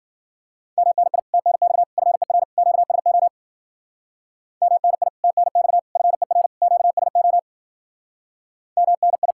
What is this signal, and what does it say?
Morse code, heard as a keyed tone, about 52 wpm, 700 Hz. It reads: MNI TNX FER QSO MNI TNX FER QSO MNI